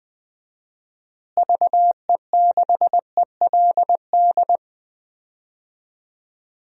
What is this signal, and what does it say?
Morse code, heard as a keyed tone, about 20 wpm, 700 Hz. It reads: VE6ELD
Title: DXCC-100 Call Signs - 6 Characters - Repeated 3x - 20wpm